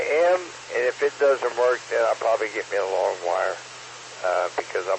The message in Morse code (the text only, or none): none